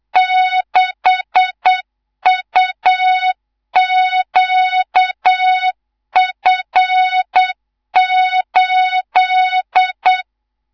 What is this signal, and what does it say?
Morse code, heard as a keyed tone, about 8 wpm, 3700 Hz, with noise. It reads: 6UQF8
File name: Morze-HA2MN / hang / 19 GYAKORLAT